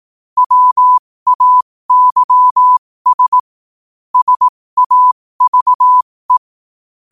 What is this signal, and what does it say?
Morse code, heard as a keyed tone, about 18 wpm, spaced as for 15 wpm, 1000 Hz. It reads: WAYS SAVE
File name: qotdfiles / 15wpm